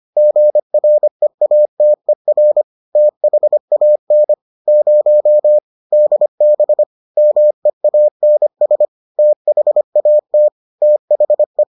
Morse code, heard as a keyed tone, about 25 wpm, 600 Hz. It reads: GREATER THAN 0 DB MEANS THAT THE